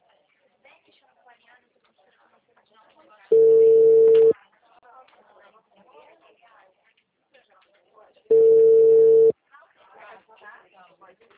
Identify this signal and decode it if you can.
Ringback tone (China, 450 Hz), with speech.